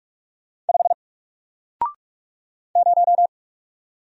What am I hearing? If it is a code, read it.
Morse code, heard as a keyed tone, about 44 wpm, 700 Hz. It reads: 5 0
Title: Single Letter-Number Alt 45wpm